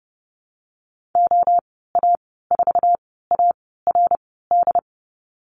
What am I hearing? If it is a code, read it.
Morse code, heard as a keyed tone, about 30 wpm, 700 Hz. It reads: O A 4 A R D
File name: Lesson 19g - Letter w - Call Signs 30wpm